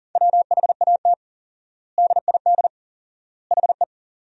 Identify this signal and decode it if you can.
Morse code, heard as a keyed tone, about 40 wpm, 700 Hz. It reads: WHAT DID HE
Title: 4x Word-Spacing - Sentences from Top 200 Words - 40wpm